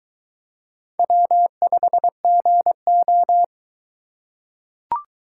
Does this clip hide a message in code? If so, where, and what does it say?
Yes: Morse code from 1.0 to 3.4 s, heard as a keyed tone, about 22 wpm, 700 Hz. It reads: W5GO